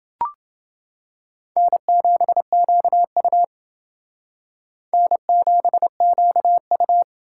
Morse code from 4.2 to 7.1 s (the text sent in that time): N7QU